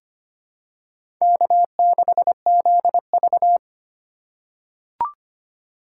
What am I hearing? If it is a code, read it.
Morse code, heard as a keyed tone, about 25 wpm, 700 Hz. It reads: K6ZV
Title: USA Call Signs - 4 Characters - 4x Character Spacing - 25wpm